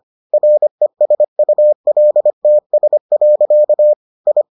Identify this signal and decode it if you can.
Morse code, heard as a keyed tone, about 25 wpm, 600 Hz. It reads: RESULTS. I